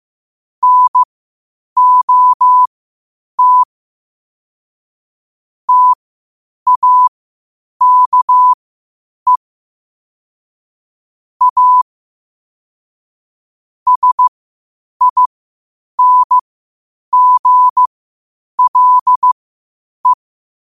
Morse code, heard as a keyed tone, about 15 wpm, spaced as for 8 wpm, 1000 Hz. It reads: NOT TAKE A SINGLE
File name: qotdfiles / 8wpm